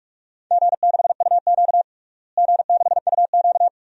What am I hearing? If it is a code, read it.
Morse code, heard as a keyed tone, about 46 wpm, 700 Hz. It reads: G6UQ G6UQ
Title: DXCC-100 Call Signs - 4 Characters - Repeated 2x - 45wpm